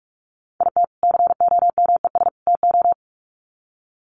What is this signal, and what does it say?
Morse code, heard as a keyed tone, about 44 wpm, 700 Hz. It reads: IT COMES TO
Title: Sets of 3 Words 45wpm